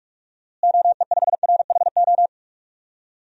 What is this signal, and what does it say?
Morse code, heard as a keyed tone, about 46 wpm, 700 Hz. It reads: OE5RHO
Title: DXCC-100 Call Signs - 6 Characters - Repeated 3x - 45wpm